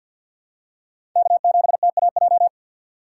Morse code, heard as a keyed tone, about 53 wpm, 700 Hz. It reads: K7TAJ
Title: USA Call Signs - 5 Characters - 4x Character Spacing - 50wpm